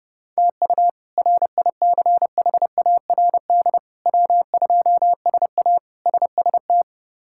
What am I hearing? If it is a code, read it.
Morse code, heard as a keyed tone, about 30 wpm, 700 Hz. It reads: TU RICHARD W2SA SST
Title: K1USN SST QSO Practice - 30wpm